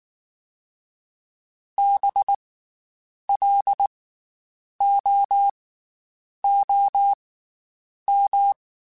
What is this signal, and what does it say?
Morse code, heard as a keyed tone, about 19 wpm, spaced as for 8 wpm, 800 Hz. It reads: BLOOM
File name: Special INGs / Special INGs-mp3s / Slow ing @18